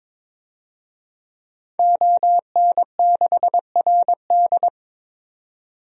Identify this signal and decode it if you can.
Morse code, heard as a keyed tone, about 22 wpm, 700 Hz. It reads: ON6RD
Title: DXCC-100 Call Signs - 5 Characters - 22wpm